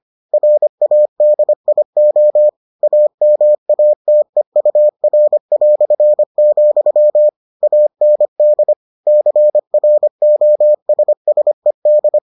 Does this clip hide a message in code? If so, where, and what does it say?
Yes: Morse code from 0.3 to 12.2 s, heard as a keyed tone, about 25 wpm, 600 Hz. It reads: RADIO AMATEUR", AND CROSSED